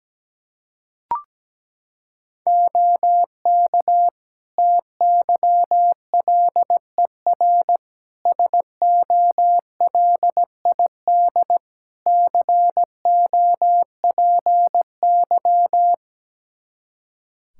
Morse code, heard as a keyed tone, about 17 wpm, 700 Hz. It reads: OK TYLER SOLID COPY
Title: Practice QSO - Exchange 3 - More Info - 17wpm